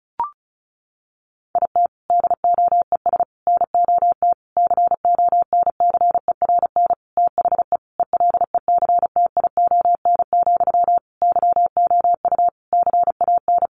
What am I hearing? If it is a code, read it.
Morse code, heard as a keyed tone, about 34 wpm, 700 Hz. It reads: IT DOES NOT CONCERN THE ELECTION, YOU CAN
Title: Sentences from Top 1500 Words 35wpm